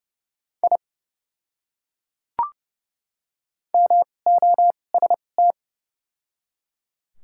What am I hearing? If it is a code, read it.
Morse code, heard as a keyed tone, about 30 wpm, spaced as for 22 wpm, 700 Hz. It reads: I MOST
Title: 2x Character Spacing - Top 100 Words - 30wpm